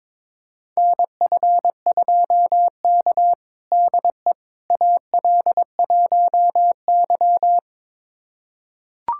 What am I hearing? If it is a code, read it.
Morse code, heard as a keyed tone, about 21 wpm, 700 Hz. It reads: NF2K DE AL1Y
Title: USA Callsign DE Callsign - 4 Characters - 22wpm